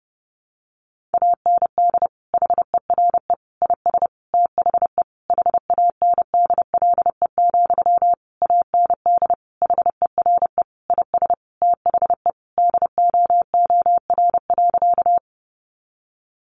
Morse code, heard as a keyed tone, about 30 wpm, 700 Hz. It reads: AND HERE IS THE HANDLE, AND HERE IS THE DOOR.
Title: Sentences from Top 1100 Words 30wpm